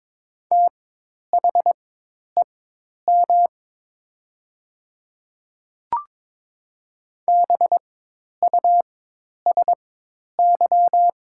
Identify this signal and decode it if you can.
Morse code, heard as a keyed tone, about 22 wpm, spaced as for 10 wpm, 700 Hz. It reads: THEM BUSY
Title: 4x Character Spacing - Top 500 Words - 22wpm